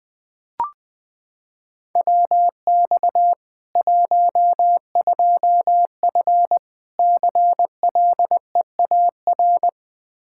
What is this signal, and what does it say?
Morse code, heard as a keyed tone, about 20 wpm, 700 Hz. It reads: WX 12F CLEAR
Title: Practice QSO - Exchange 2 - Info - 20wpm